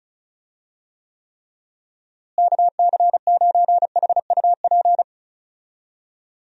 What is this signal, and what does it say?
Morse code, heard as a keyed tone, about 36 wpm, 700 Hz. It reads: KC9HUP
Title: USA Call Signs - 6 Characters - Repeated 3x - 35wpm